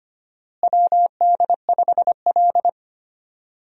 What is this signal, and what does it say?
Morse code, heard as a keyed tone, about 25 wpm, 700 Hz. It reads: WD5L
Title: USA Call Signs - 4 Characters - Repeated 2x - 25wpm